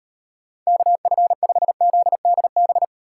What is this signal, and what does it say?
Morse code, heard as a keyed tone, about 34 wpm, 700 Hz. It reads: KF5ZDB